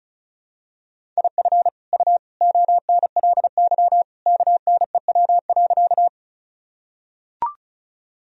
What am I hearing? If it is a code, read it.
Morse code, heard as a keyed tone, about 36 wpm, 700 Hz. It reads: IF U ONLY KNEW.
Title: Sentences from Top 400 Words and Abbreviations 35wpm